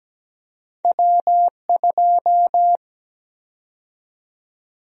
Morse code, heard as a keyed tone, about 17 wpm, 700 Hz. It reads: W2
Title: USA Call Sign Prefixes - 17wpm